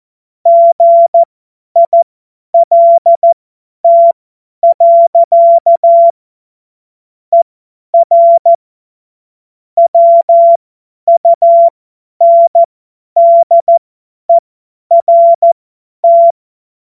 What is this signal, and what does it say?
Morse code, heard as a keyed tone, about 14 wpm, spaced as for 10 wpm, 680 Hz. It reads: GILT. ER WUNDERT
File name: Morsekurs / Witze / Witz 04. Januar 2022, 50 F70